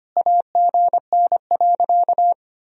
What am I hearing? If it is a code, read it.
Morse code, heard as a keyed tone, about 25 wpm, 700 Hz. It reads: AGN.
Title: Sentences from Top 300 Words and Abbreviations 25wpm